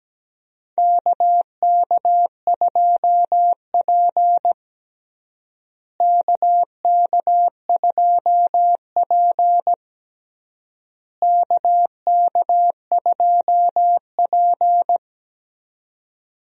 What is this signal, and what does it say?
Morse code, heard as a keyed tone, about 17 wpm, 700 Hz. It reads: KK2P KK2P KK2P
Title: USA Call Signs - 4 Characters - Repeated 3x - 17wpm